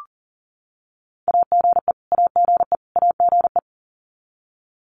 Morse code, heard as a keyed tone, about 40 wpm, 700 Hz. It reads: AGE AGE AGE
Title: Top 500 Words Repeated 40wpm